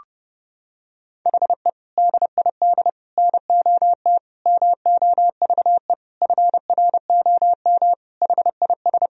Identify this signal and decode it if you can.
Morse code, heard as a keyed tone, about 30 wpm, 700 Hz. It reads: HE DID NOT MOVE FROM HIS
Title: Sentences from Top 1000 Words - Review Part 1 - 30wpm